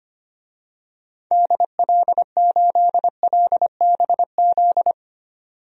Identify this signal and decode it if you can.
Morse code, heard as a keyed tone, about 25 wpm, 700 Hz. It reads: DL8LBZ
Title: DXCC-100 Call Signs - 6 Characters - 25wpm